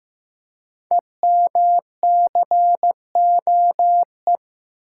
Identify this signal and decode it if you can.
Morse code, heard as a keyed tone, about 15 wpm, 700 Hz. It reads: EMCOE
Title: Code Groups - 5 Letter:Numbers - 15wpm